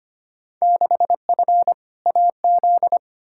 Morse code, heard as a keyed tone, about 25 wpm, 700 Hz. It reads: 6F AZ
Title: Field Day Exchanges Encore - 25wpm